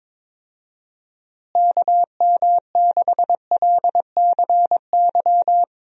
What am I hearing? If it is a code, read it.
Morse code, heard as a keyed tone, about 22 wpm, 700 Hz. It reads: KM6LCY
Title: USA Call Signs - 6 Characters - Repeated 2x - 22wpm